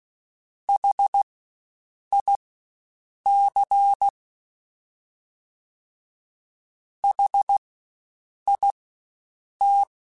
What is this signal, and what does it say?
Morse code, heard as a keyed tone, about 16 wpm, spaced as for 7 wpm, 800 Hz. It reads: HIC HIT
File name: Spaced Alphas / Spaced Alphas mp3s / Hs sp @15